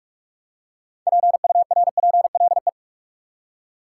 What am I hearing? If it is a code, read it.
Morse code, heard as a keyed tone, about 43 wpm, 700 Hz. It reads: PURPLE
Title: Six Letter Words - 45wpm